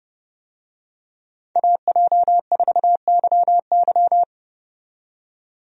Morse code, heard as a keyed tone, about 29 wpm, 700 Hz. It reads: AJ4YY
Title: USA Call Signs - 5 Characters - 30wpm